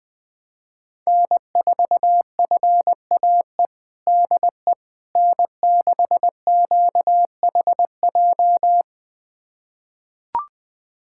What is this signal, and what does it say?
Morse code, heard as a keyed tone, about 21 wpm, 700 Hz. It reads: N4FAE DE N6QHJ
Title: USA Callsign DE Callsign - 5 Characters - 20wpm